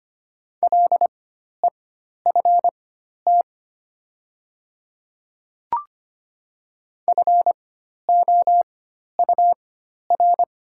Morse code, heard as a keyed tone, about 25 wpm, spaced as for 12 wpm, 700 Hz. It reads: LEFT FOUR
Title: 4x Character Spacing - Top 300 Words - 25wpm